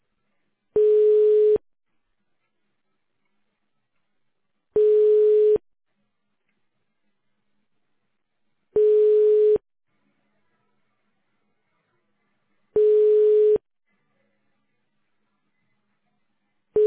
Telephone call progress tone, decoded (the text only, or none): ringback tone